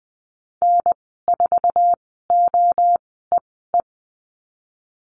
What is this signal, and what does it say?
Morse code, heard as a keyed tone, about 20 wpm, spaced as for 15 wpm, 700 Hz. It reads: N4OEE
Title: 055 Lesson 10g - Letter r - Call Signs 20wpm